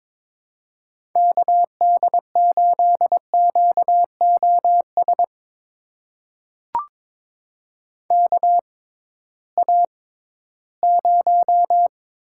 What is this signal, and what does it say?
Morse code, heard as a keyed tone, about 22 wpm, spaced as for 14 wpm, 700 Hz. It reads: KD8QOS KA0
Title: USA Call Signs - 6 Characters - 6x Character Spacing - 22wpm